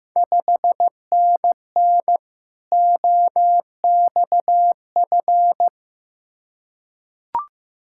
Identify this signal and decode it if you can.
Morse code, heard as a keyed tone, about 15 wpm, 700 Hz. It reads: 5NN OXF
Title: Ontario QSO Party Exchange 15wpm